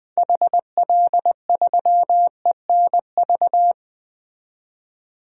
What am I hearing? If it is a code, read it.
Morse code, heard as a keyed tone, about 20 wpm, 700 Hz. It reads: HL3ENV
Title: DXCC-100 Call Signs - 6 Characters - 20wpm